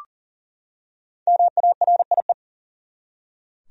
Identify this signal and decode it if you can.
Morse code, heard as a keyed tone, about 39 wpm, 700 Hz. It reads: MARIE
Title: USA Common Names - 40wpm